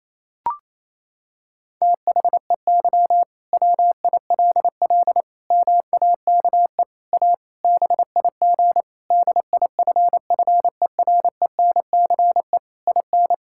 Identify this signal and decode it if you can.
Morse code, heard as a keyed tone, about 28 wpm, 700 Hz. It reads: THEY WILL MAKE A BIG DIFFERENCE IN
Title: Sentences from Top 1000 Words - Review Part 5 - 28wpm